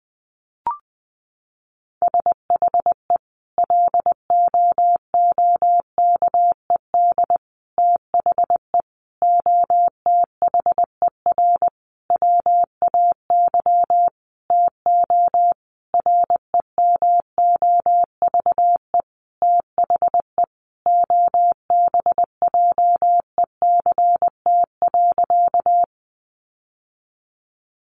Morse code, heard as a keyed tone, about 20 wpm, 700 Hz. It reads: SHE LOOKED THE OTHER WAY TO REMOVE THE OBJECT.
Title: Sentences from Top 800 Words 20wpm